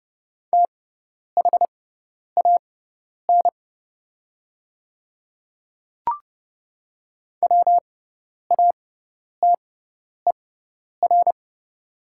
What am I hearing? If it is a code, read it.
Morse code, heard as a keyed tone, about 30 wpm, spaced as for 10 wpm, 700 Hz. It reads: THAN WATER
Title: 6x Character Spacing - Top 100 Words - 30wpm